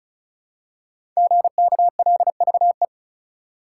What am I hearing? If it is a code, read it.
Morse code, heard as a keyed tone, about 36 wpm, 700 Hz. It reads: GKLVE
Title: Code Groups - 5 Letter:Numbers - 35wpm